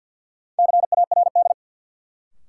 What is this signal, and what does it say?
Morse code, heard as a keyed tone, about 53 wpm, 700 Hz. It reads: CARD
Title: Sentences from Top 600 Words - New Word Drill - 50wpm